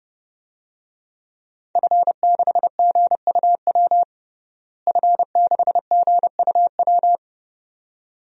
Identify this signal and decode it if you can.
Morse code, heard as a keyed tone, about 30 wpm, 700 Hz. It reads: F6GUW F6GUW
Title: DXCC-100 Call Signs - 5 Characters - Repeated 2x - 30wpm